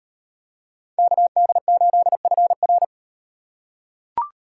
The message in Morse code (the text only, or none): KD8FR